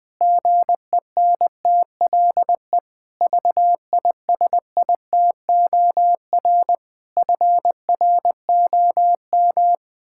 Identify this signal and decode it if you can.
Morse code, heard as a keyed tone, about 20 wpm, 700 Hz. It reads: GENTLE VISITOR FROM
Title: Pangram Challenge v2 - 20wpm